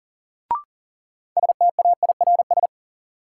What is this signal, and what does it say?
Morse code, heard as a keyed tone, about 40 wpm, 700 Hz. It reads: STAIRS
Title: Sets of 5 Words Encore - New Word Drill - 40wpm